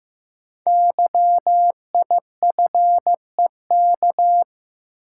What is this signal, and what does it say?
Morse code, heard as a keyed tone, about 15 wpm, 700 Hz. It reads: YIFEK
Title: Code Groups - 5 Letters - 15wpm